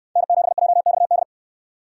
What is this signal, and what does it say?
Morse code, heard as a keyed tone, about 68 wpm, 700 Hz. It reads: N7JXD